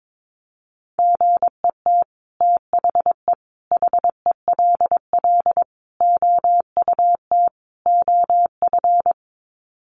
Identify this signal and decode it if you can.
Morse code, heard as a keyed tone, about 22 wpm, 700 Hz. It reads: GET THE HELL OUT OF
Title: Sets of 5 Words Encore2 22wpm